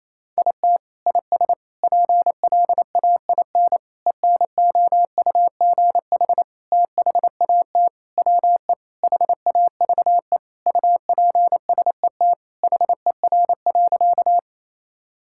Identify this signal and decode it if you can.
Morse code, heard as a keyed tone, about 28 wpm, 700 Hz. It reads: IT IS PLAIN ENOUGH THAT WE HAVE UPSET HER.
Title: Sentences from Top 1000 Words - Review Part 5 - 28wpm